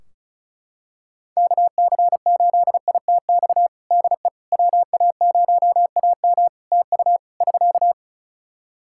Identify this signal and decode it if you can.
Morse code, heard as a keyed tone, about 36 wpm, 700 Hz. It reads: KC8ITX DE WA0AM TU <SK>
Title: Practice QSO - Exchange 4 - Ending - 35wpm